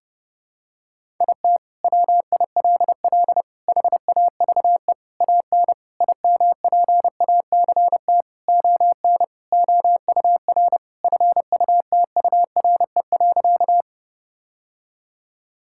Morse code, heard as a keyed tone, about 30 wpm, 700 Hz. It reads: IT WILL HAVE AN IMPACT ON OUR FUTURE.
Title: Sentences from Top 1000 Words - Review Part 6 - 30wpm